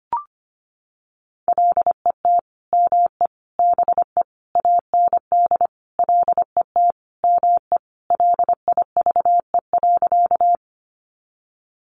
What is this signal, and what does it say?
Morse code, heard as a keyed tone, about 25 wpm, 700 Hz. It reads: LET ME BE AND LET ME LIVE.
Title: Sentences from Top 300 Words 25wpm